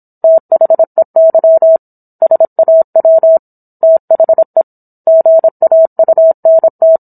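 Morse code, heard as a keyed tone, about 27 wpm, 650 Hz. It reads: THEY SAW THE GAUNT
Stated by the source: Morse code practice of a book chapter at 26 WPM.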